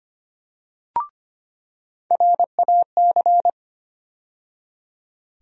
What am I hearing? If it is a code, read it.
Morse code, heard as a keyed tone, about 25 wpm, 700 Hz. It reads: RAC
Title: USA Call Sign Postfixes - 3 Letters 25wpm